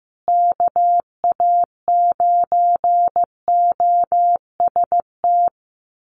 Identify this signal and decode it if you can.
Morse code, heard as a keyed tone, about 15 wpm, 700 Hz. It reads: KA9OST
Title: USA Call Signs Repeated 2x 15wpm